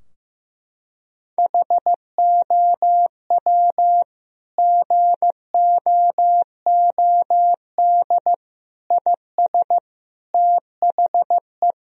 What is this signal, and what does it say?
Morse code, heard as a keyed tone, about 15 wpm, 700 Hz. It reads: HOW GOOD IS THE
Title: Sentences from Top 1000 Words - Review Part 4 - 15wpm